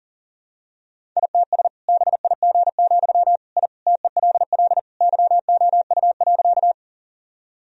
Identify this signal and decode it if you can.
Morse code, heard as a keyed tone, about 40 wpm, 700 Hz. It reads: ITS BIG, I TELL YOU.
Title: Sentences from Top 1000 Words - Review Part 1 - 40wpm